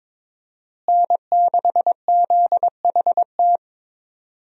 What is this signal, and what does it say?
Morse code, heard as a keyed tone, about 22 wpm, 700 Hz. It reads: N6ZHT
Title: USA Call Signs - 5 Characters - Repeated 3x - 22wpm